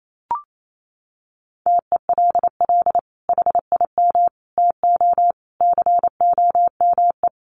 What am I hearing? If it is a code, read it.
Morse code, heard as a keyed tone, about 28 wpm, 700 Hz. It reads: TELL HIM TO COME
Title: Sentences from Top 700 Words 28wpm